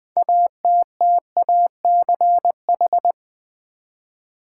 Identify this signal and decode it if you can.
Morse code, heard as a keyed tone, about 20 wpm, 700 Hz. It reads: ATTACH
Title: Six Letter Words - 20wpm